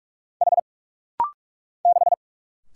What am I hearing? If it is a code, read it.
Morse code, heard as a keyed tone, about 44 wpm, 690 Hz. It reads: H 6
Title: Single Letter-Number - Rapid-Fire 45wpm